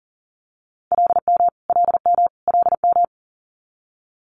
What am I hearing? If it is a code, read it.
Morse code, heard as a keyed tone, about 40 wpm, 700 Hz. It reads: LM LM LM
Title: Multi-letter Phonemes Repeated 3x 40wpm